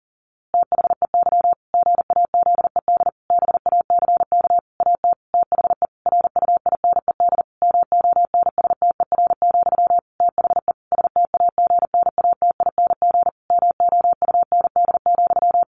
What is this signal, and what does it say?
Morse code, heard as a keyed tone, about 37 wpm, 700 Hz. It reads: THEY GAZED BACK AT THE RUINED MONSTER, THE STAGNATING MOUND,
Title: A Sound Of Thunder 40wpm